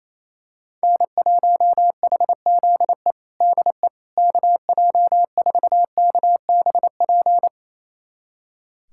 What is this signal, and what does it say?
Morse code, heard as a keyed tone, about 28 wpm, 700 Hz. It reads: N1HZE DE KJ4KBP
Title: USA Callsign DE Callsign 28wpm